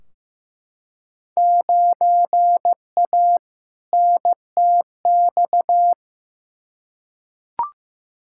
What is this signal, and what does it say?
Morse code, heard as a keyed tone, about 15 wpm, 700 Hz. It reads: 9A NTX
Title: Field Day Exchanges Encore - 15wpm